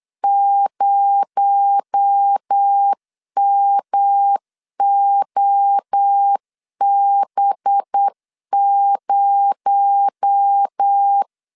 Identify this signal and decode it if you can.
Morse code, heard as a keyed tone, about 8 wpm, 800 Hz. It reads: 0MOB0
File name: LEZIONI / 6 PKX0 / 6 PKX0 8wpm 3341